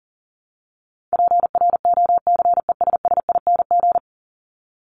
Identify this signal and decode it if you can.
Morse code, heard as a keyed tone, about 40 wpm, 700 Hz. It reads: PROCESSING
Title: ING Suffix - Top 100 Words with Base Words 40wpm